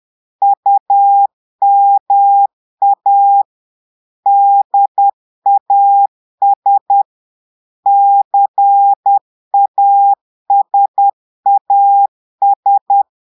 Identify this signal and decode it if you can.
Morse code, heard as a keyed tone, about 10 wpm, 800 Hz. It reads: UMA DAS CASAS
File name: Portugais - texte en clair 1 - vitesse - 10 mots minute